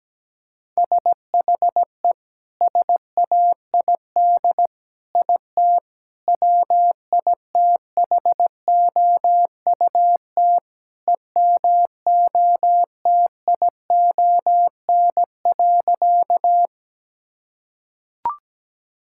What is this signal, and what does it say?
Morse code, heard as a keyed tone, about 17 wpm, 700 Hz. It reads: SHE SAID IT WITHOUT EMOTION.
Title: Sentences from Top 1600 Words 17wpm